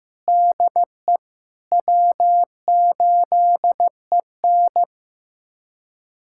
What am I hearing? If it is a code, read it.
Morse code, heard as a keyed tone, about 15 wpm, 700 Hz. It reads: DE W8EN
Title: USA Callsign DE Callsign - 4 Characters - 15wpm